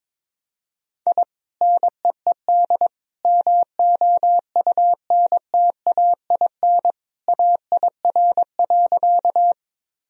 Morse code, heard as a keyed tone, about 22 wpm, 700 Hz. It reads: I NEED MOUNTAIN AIR.
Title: Sentences from Top 1000 Words - Review Part 1 - 22wpm